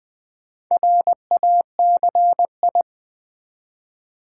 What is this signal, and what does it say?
Morse code, heard as a keyed tone, about 20 wpm, 700 Hz. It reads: RACI
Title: Build A Word - 2 Letters At A Time - 6-Letter Words - 20wpm